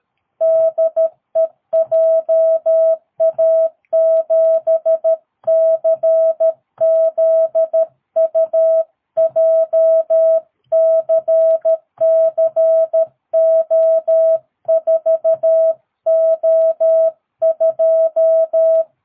Morse code, heard as a keyed tone, about 13 wpm, 650 Hz. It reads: DE JA7CZU JCCO4O2
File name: jh8cbh / 2022cw